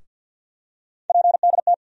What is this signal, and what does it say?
Morse code, heard as a keyed tone, about 51 wpm, 700 Hz. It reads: PDT
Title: USA Call Sign Postfixes - 3 Letters 50wpm